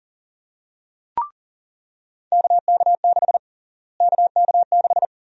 Morse code, heard as a keyed tone, about 40 wpm, 700 Hz. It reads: KK6 KK6
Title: USA Call Sign Prefixes - Repeated 2x - 40wpm